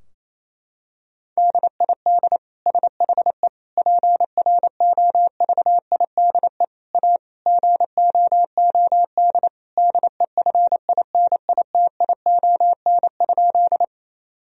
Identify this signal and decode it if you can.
Morse code, heard as a keyed tone, about 28 wpm, 700 Hz. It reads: DID SHE PROVIDE A GOOD DEFINITION?
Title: Sentences from Top 1000 Words - Review Part 4 - 28wpm